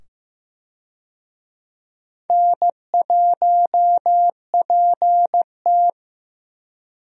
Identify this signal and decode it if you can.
Morse code, heard as a keyed tone, about 15 wpm, 700 Hz. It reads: N1PT
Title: USA Call Signs - 4 Characters - 15wpm